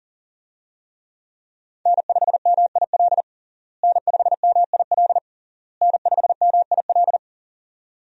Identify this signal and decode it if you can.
Morse code, heard as a keyed tone, about 40 wpm, 700 Hz. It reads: N5MIL N5MIL N5MIL
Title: USA Call Signs - 5 Characters - Repeated 3x - 40wpm